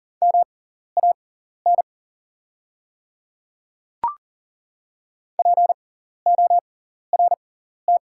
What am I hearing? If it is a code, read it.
Morse code, heard as a keyed tone, about 40 wpm, spaced as for 14 wpm, 690 Hz. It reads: MAN PORT
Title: 6x Character Spacing - Top 500 Words - 40wpm